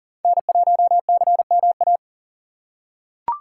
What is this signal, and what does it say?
Morse code, heard as a keyed tone, about 39 wpm, 700 Hz. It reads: N1CMA
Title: USA Call Signs - 5 Characters - 2x Character Spacing - 40wpm